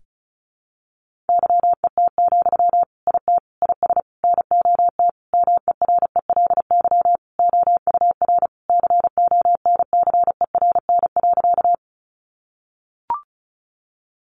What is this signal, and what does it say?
Morse code, heard as a keyed tone, about 34 wpm, 700 Hz. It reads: YET, IT IS NOT MERELY OUR CONCERN.
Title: Sentences from Top 1400 Words 35wpm